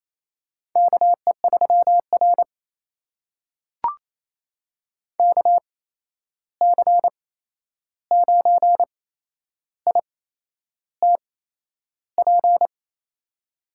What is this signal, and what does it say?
Morse code, heard as a keyed tone, about 28 wpm, spaced as for 11 wpm, 700 Hz. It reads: KE3R KC9ITP
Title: USA Call Signs - 8x Character Spacing - 28wpm